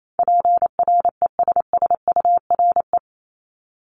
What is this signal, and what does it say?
Morse code, heard as a keyed tone, about 28 wpm, 700 Hz. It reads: PRESSURE
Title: Multi-letter Phonemes Repeated 3x 28wpm